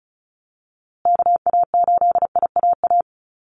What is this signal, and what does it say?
Morse code, heard as a keyed tone, about 34 wpm, 700 Hz. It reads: KA8IAA
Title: USA Call Signs Repeated 3x 35wpm